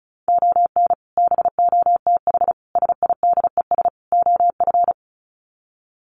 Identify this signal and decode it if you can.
Morse code, heard as a keyed tone, about 34 wpm, 700 Hz. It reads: ON BOTH SIDES OF
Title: Sets of 4 Words 35wpm